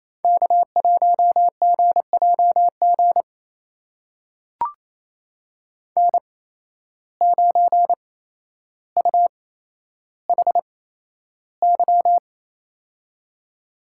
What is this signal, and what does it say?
Morse code, heard as a keyed tone, about 28 wpm, spaced as for 12 wpm, 700 Hz. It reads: K1GJG N9UHY
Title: USA Call Signs - 5 Characters - 8x Character Spacing - 28wpm